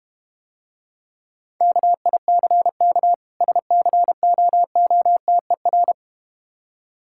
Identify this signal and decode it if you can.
Morse code, heard as a keyed tone, about 31 wpm, 700 Hz. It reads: KICK SCOOTER
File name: practice / 25wpm / Difficult Words - Speed-Racing with Hints - 25wpm